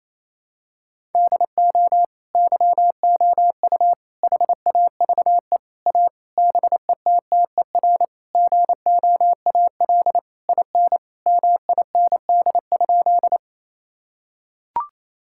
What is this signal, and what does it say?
Morse code, heard as a keyed tone, about 28 wpm, 700 Hz. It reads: DO YOU HAVE A BETTER GOAL IN MIND?
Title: Sentences from Top 1000 Words - Review Part 5 - 28wpm